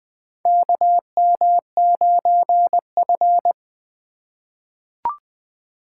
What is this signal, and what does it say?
Morse code, heard as a keyed tone, about 20 wpm, 700 Hz. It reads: KM9F
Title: USA Call Signs - 4 Characters - 4x Character Spacing - 20wpm